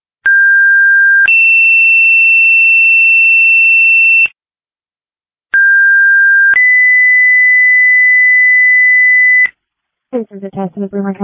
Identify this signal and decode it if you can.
Two-tone page, heard as a one-second tone, then a three-second tone, about 1600 then 2700 Hz, about 1600 then 2000 Hz, with speech. It is heard from 0.2 to 9.5 s.